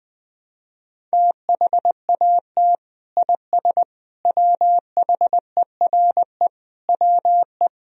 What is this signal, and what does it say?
Morse code, heard as a keyed tone, about 21 wpm, 700 Hz. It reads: THAT IS WHERE WE
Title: Sentences from Top 1000 Words - Review Part 1 - 20wpm